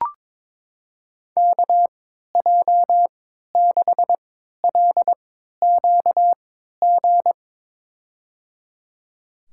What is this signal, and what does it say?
Morse code, heard as a keyed tone, about 22 wpm, spaced as for 13 wpm, 700 Hz. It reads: KJ6LQG
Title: USA Call Signs - 6 Characters - 3x Character Spacing - 22wpm